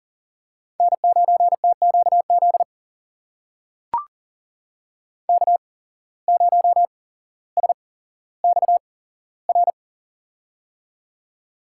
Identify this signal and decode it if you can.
Morse code, heard as a keyed tone, about 40 wpm, spaced as for 17 wpm, 700 Hz. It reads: N9TQZ K0SXR
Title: USA Call Signs - 5 Characters - 8x Character Spacing - 40wpm